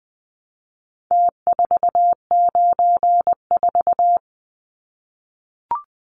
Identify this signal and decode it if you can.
Morse code, heard as a keyed tone, about 20 wpm, 700 Hz. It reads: T494
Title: Numbers - Contest Style - 4-Digit 20wpm